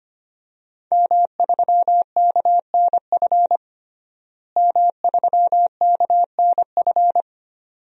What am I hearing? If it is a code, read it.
Morse code, heard as a keyed tone, about 25 wpm, 700 Hz. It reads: M3KNF M3KNF
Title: DXCC-100 Call Signs - 5 Characters - Repeated 2x - 25wpm